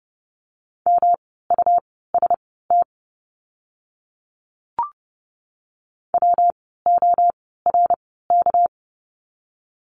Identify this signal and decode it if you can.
Morse code, heard as a keyed tone, about 30 wpm, spaced as for 17 wpm, 700 Hz. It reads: MUST WORK
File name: course / 30:10wpm / 237 Lesson 38f - Letter q - Words 30wpm